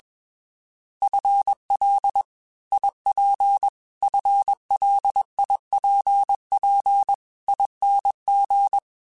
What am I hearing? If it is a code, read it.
Morse code, heard as a keyed tone, about 21 wpm, 800 Hz. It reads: FL IP FLIPP ING
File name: FL / FL mp3s / Fast FL @20